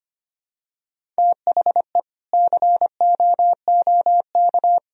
Morse code, heard as a keyed tone, about 25 wpm, 700 Hz. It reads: THE COOK
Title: Sentences from Top 1000 Words - Review Part 4 - 25wpm